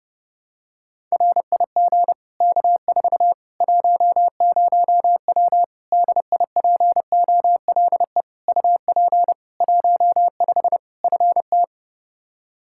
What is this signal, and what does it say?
Morse code, heard as a keyed tone, about 32 wpm, 700 Hz. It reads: RIG K4 10W DIPOLE UP 15 FT
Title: Practice QSO - Exchange 2 - Info - 30wpm